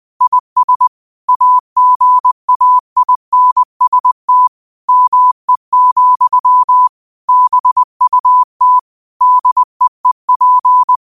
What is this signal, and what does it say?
Morse code, heard as a keyed tone, about 20 wpm, 1000 Hz. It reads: IS AGAINST ME, BUT DEEP